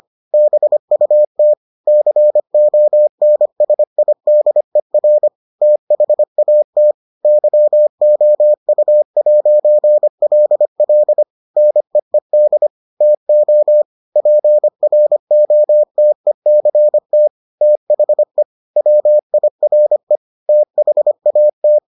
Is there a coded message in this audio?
Yes: Morse code from 0.3 to 21.8 s, heard as a keyed tone, about 25 wpm, 600 Hz. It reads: BUT CONSIDER THAT YOU'LL NEED TO PROTECT THE WIRE THAT